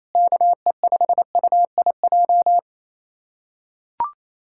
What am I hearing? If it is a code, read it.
Morse code, heard as a keyed tone, about 28 wpm, 700 Hz. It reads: KE5UIJ
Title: USA Callsign DE Callsign 28wpm